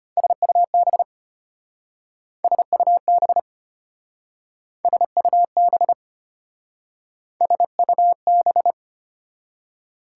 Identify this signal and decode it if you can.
Morse code, heard as a keyed tone, about 32 wpm, 700 Hz. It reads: SUB SUB SUB SUB